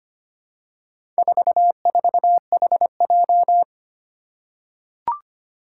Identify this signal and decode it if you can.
Morse code, heard as a keyed tone, about 25 wpm, 700 Hz. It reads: 44HJ
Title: Four Challenging Letter-Number Combinations 25wpm